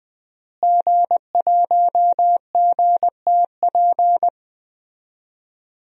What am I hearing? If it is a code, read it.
Morse code, heard as a keyed tone, about 20 wpm, 700 Hz. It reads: G1GTP
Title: DXCC-100 Call Signs - 5 Characters - Repeated 3x - 20wpm